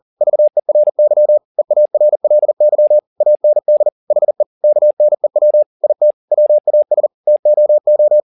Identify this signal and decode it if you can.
Morse code, heard as a keyed tone, about 40 wpm, 600 Hz. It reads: VERY EARLY AND HE KNEW IT WAS TOO